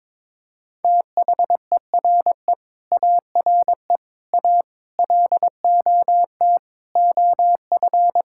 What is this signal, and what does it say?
Morse code, heard as a keyed tone, about 22 wpm, 700 Hz. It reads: THERE ARE A LOT OF
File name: practice / 22wpm / Sets of 8 Words - 22wpm